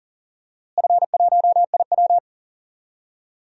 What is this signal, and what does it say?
Morse code, heard as a keyed tone, about 40 wpm, 700 Hz. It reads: F1IW
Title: DXCC-100 Call Signs - 4 Characters - Repeated 3x - 40wpm